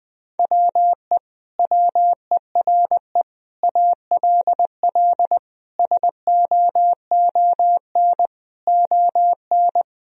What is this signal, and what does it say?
Morse code, heard as a keyed tone, about 20 wpm, 700 Hz. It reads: WE WERE ALL SOON ON
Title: Sentences from Top 1000 Words - Review Part 4 - 20wpm